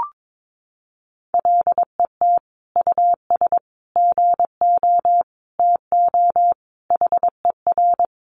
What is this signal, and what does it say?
Morse code, heard as a keyed tone, about 22 wpm, 700 Hz. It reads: LET US GO TO HER
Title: Sentences from Top 1100 Words 22wpm